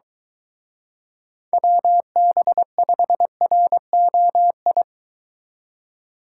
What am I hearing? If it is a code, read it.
Morse code, heard as a keyed tone, about 23 wpm, 700 Hz. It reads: WB5ROI